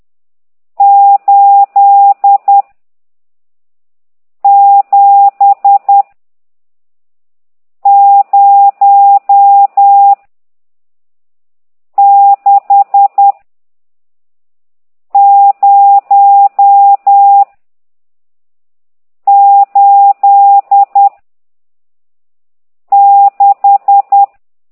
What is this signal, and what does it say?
Morse code, heard as a keyed tone, about 10 wpm, 800 Hz. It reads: 8 7 0 6 0 8 6
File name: ct2idw / exerc14